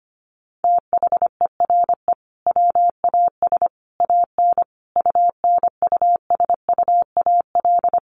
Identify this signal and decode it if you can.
Morse code, heard as a keyed tone, about 25 wpm, 700 Hz. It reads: THERE WAS AN UNUSUAL
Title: Sentences from Top 1400 Words 25wpm